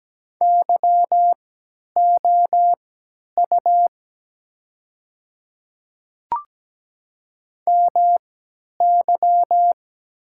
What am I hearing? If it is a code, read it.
Morse code, heard as a keyed tone, about 17 wpm, spaced as for 10 wpm, 700 Hz. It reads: YOU MY
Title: 3x Character Spacing - Top 300 Words - 17wpm